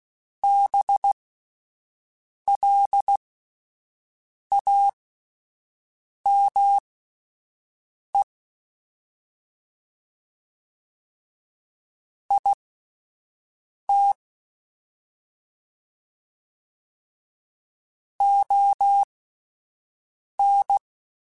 Morse code, heard as a keyed tone, about 16 wpm, spaced as for 5 wpm, 800 Hz. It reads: BLAME IT ON
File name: Spaced / Chuckles Spaced-mp3s / HumanErr sp @10